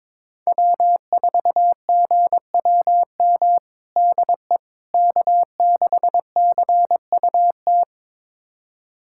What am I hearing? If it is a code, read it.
Morse code, heard as a keyed tone, about 22 wpm, 700 Hz. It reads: W4GWM DE K6CUT
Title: USA Callsign DE Callsign - 5 Characters - 22wpm